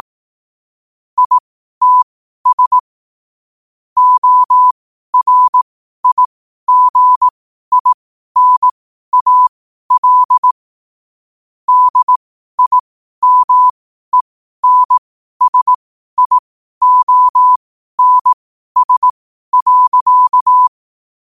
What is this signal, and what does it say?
Morse code, heard as a keyed tone, about 18 wpm, spaced as for 12 wpm, 1000 Hz. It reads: ITS ORIGINAL DIMENSIONS.
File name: qotdfiles / 12wpm